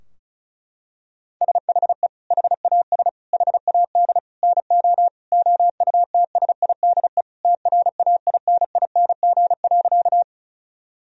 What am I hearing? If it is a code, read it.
Morse code, heard as a keyed tone, about 36 wpm, 700 Hz. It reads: SHE HAS HAD NO OUTSIDE TRAINING.
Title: Sentences from Top 1000 Words - Review Part 4 - 35wpm